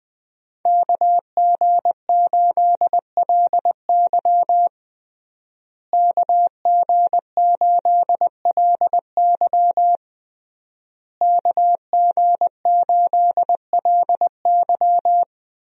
Morse code, heard as a keyed tone, about 21 wpm, 700 Hz. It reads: KG8LY KG8LY KG8LY
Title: USA Call Signs - 5 Characters - Repeated 3x - 20wpm